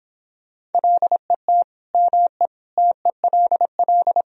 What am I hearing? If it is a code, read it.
Morse code, heard as a keyed tone, about 29 wpm, 700 Hz. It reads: LET ME TELL